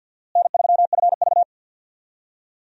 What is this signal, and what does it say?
Morse code, heard as a keyed tone, about 52 wpm, 700 Hz. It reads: N3FV
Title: USA Call Signs - 4 Characters - Repeated 3x - 50wpm